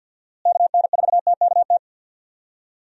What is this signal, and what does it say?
Morse code, heard as a keyed tone, about 52 wpm, 700 Hz. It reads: KN4TKT
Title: USA Call Signs - 6 Characters - Repeated 3x - 50wpm